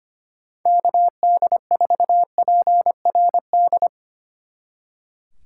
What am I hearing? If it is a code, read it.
Morse code, heard as a keyed tone, about 29 wpm, 700 Hz. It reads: KD4PRD
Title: Speed Racing - USA 2x3 Call Signs 25wpm